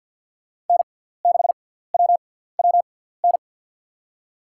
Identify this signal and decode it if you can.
Morse code, heard as a keyed tone, about 52 wpm, spaced as for 17 wpm, 700 Hz. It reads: N6WWN
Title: USA Call Signs - 6x Character Spacing - 50wpm